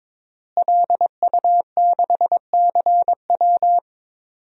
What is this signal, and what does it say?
Morse code, heard as a keyed tone, about 22 wpm, 700 Hz. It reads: LU6CW
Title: DXCC-100 Call Signs - 5 Characters - Repeated 2x - 22wpm